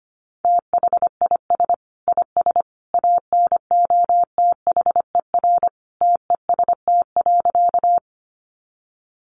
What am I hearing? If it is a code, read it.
Morse code, heard as a keyed tone, about 25 wpm, 700 Hz. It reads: THIS IS ANOTHER TEST.
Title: Sentences from Top 600 Words 25wpm